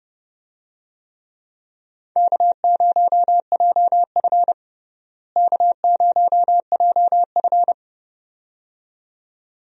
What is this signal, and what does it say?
Morse code, heard as a keyed tone, about 30 wpm, 700 Hz. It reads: K0JF K0JF
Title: USA Call Signs - 4 Characters - Repeated 2x - 30wpm